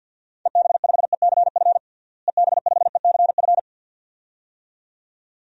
Morse code, heard as a keyed tone, about 53 wpm, 700 Hz. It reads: EB5ECF EB5ECF
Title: DXCC-100 Call Signs - 6 Characters - Repeated 2x - 50wpm